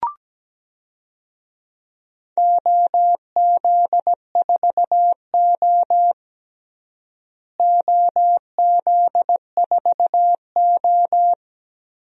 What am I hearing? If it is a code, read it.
Morse code, heard as a keyed tone, about 17 wpm, 700 Hz. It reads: OZ4O OZ4O
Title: DXCC-100 Call Signs - 4 Characters - Repeated 2x - 17wpm